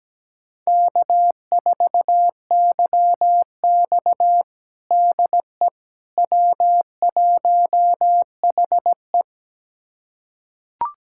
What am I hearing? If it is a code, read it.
Morse code, heard as a keyed tone, about 17 wpm, 700 Hz. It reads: K4YX DE W1HE
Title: USA Callsign DE Callsign - 4 Characters - 17wpm